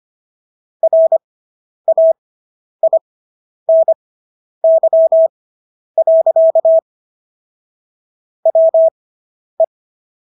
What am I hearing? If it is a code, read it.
Morse code, heard as a keyed tone, about 25 wpm, spaced as for 10 wpm, 650 Hz. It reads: RAINY. WE